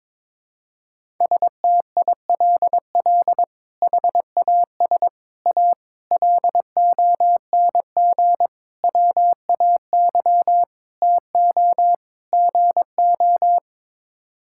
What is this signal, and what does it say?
Morse code, heard as a keyed tone, about 22 wpm, 700 Hz. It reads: STILL HAS A LONG WAY TO GO
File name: practice / 22wpm / Sets of 7 Words - 22wpm